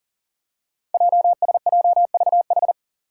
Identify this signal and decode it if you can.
Morse code, heard as a keyed tone, about 38 wpm, 700 Hz. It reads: JSJVH
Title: Five Challenging Letter-Number Combinations 40wpm